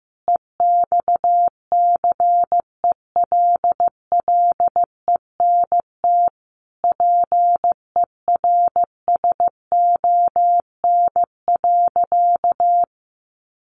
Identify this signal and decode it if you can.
Morse code, heard as a keyed tone, about 15 wpm, 700 Hz. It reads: EXCELLENT PERSON.
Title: Sentences from Top 1000 Words 15wpm